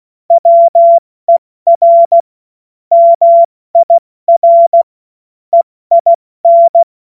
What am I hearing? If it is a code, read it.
Morse code, heard as a keyed tone, about 16 wpm, spaced as for 14 wpm, 680 Hz. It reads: WER MIR EIN
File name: Morsekurs / Witze / Witz 09. November 2021, 70 F80